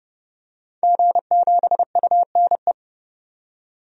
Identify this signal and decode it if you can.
Morse code, heard as a keyed tone, about 29 wpm, 700 Hz. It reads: G7UNE
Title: DXCC-100 Call Signs - 5 Characters - 30wpm